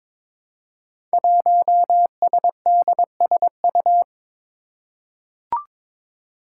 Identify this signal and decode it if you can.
Morse code, heard as a keyed tone, about 22 wpm, 700 Hz. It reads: 1SDSU
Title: Five Challenging Letter-Number Combinations 22wpm